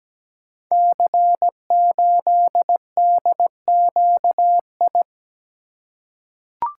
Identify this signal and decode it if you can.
Morse code, heard as a keyed tone, about 17 wpm, 700 Hz. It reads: C8DQI
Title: Code Groups - 5 Letter:Numbers - 17wpm